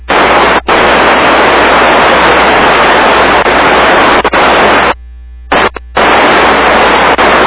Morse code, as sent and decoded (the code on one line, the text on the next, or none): none